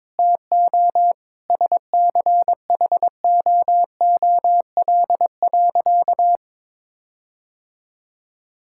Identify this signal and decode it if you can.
Morse code, heard as a keyed tone, about 21 wpm, 700 Hz. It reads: TO SCHOOL.
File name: practice / 22wpm / Sentences with 4-Word Phrases - 22wpm